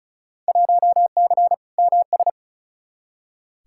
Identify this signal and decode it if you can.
Morse code, heard as a keyed tone, about 36 wpm, 700 Hz. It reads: 1C MS
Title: Field Day Exchanges Encore - 35wpm